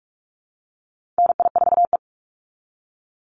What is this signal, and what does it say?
Morse code, heard as a keyed tone, about 44 wpm, 700 Hz. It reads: NI4E
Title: Call Signs Repeated 2x - 45wpm